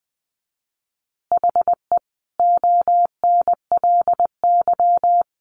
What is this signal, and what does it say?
Morse code, heard as a keyed tone, about 20 wpm, 700 Hz. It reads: HE ONLY
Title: Additional Easy Phrases 20wpm